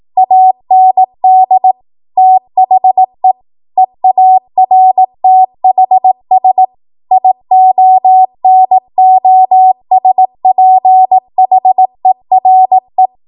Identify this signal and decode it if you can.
Morse code, heard as a keyed tone, about 18 wpm, 750 Hz. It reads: AND THE EARTHS IONOSPHERE
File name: Archive / 18 WPM / 18WPM